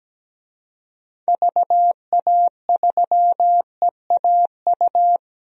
Morse code, heard as a keyed tone, about 17 wpm, 700 Hz. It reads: VA3EAU
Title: DXCC-100 Call Signs - 6 Characters - 17wpm